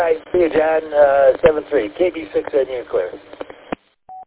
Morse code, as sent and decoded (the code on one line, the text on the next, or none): none